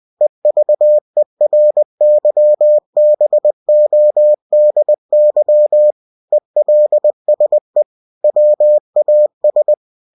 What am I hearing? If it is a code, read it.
Morse code, heard as a keyed tone, about 20 wpm, 600 Hz. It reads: EVERYBODY ELSE WAS
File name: cw-books / winnie-the-pooh / winnie-20wpm-page-0013